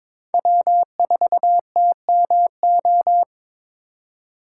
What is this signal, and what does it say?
Morse code, heard as a keyed tone, about 22 wpm, 700 Hz. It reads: W4TMO
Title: K1USN SST QSO Practice - 22wpm